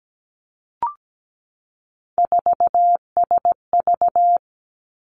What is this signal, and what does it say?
Morse code, heard as a keyed tone, about 17 wpm, 700 Hz. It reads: 4SV
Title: Three Challenging Letter-Number Combinations 17wpm